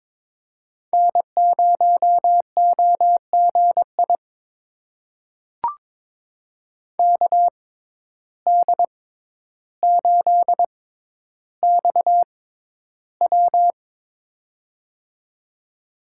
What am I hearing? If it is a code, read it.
Morse code, heard as a keyed tone, about 22 wpm, spaced as for 11 wpm, 700 Hz. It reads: N0OGI KD8XW
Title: USA Call Signs - 5 Characters - 6x Character Spacing - 22wpm